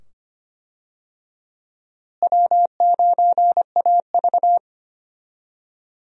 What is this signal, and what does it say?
Morse code, heard as a keyed tone, about 25 wpm, 700 Hz. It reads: W9AV
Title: USA Call Signs - 4 Characters - Repeated 2x - 25wpm